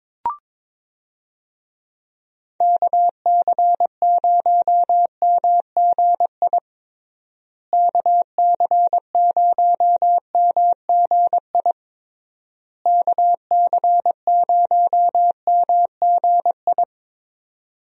Morse code, heard as a keyed tone, about 22 wpm, 700 Hz. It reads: KC0MGI KC0MGI KC0MGI
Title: USA Call Signs - 6 Characters - Repeated 3x - 22wpm